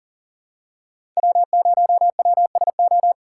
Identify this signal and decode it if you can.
Morse code, heard as a keyed tone, about 38 wpm, 700 Hz. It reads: W0WSO
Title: USA Call Signs - 5 Characters - 2x Character Spacing - 40wpm